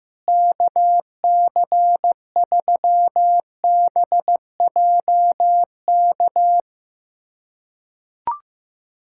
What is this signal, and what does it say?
Morse code, heard as a keyed tone, about 15 wpm, 700 Hz. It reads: KC3BJK
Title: USA Callsign DE Callsign - 6 Characters - 15wpm